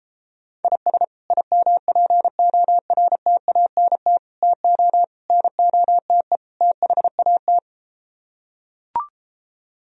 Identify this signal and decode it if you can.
Morse code, heard as a keyed tone, about 34 wpm, 700 Hz. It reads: IS IMPORTANT TO NOTE THAT